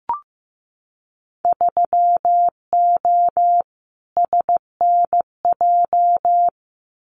Morse code, heard as a keyed tone, about 15 wpm, 700 Hz. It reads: 3O SNJ
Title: Winter Field Day Exchanges Encore 15wpm